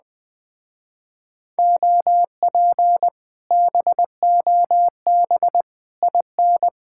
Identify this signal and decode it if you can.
Morse code, heard as a keyed tone, about 20 wpm, 700 Hz. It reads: OP BOB IN